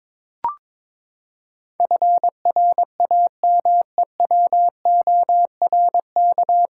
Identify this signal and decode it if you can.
Morse code, heard as a keyed tone, about 22 wpm, 700 Hz. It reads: FRAMEWORK
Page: Morse Code Ninja practice 22wpm